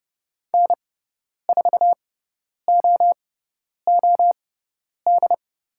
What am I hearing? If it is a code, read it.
Morse code, heard as a keyed tone, about 30 wpm, spaced as for 10 wpm, 700 Hz. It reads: N4OOD
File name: Lesson 17g - Letter u - Call Signs 30wpm